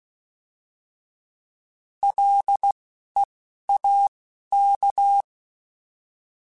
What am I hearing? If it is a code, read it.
Morse code, heard as a keyed tone, about 16 wpm, 800 Hz. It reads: L E A K